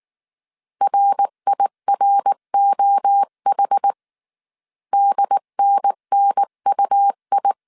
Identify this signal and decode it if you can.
Morse code, heard as a keyed tone, about 19 wpm, 800 Hz. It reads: RIROH DNNUI
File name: CW / DET / 1 3 1500C S B 18wpm 1916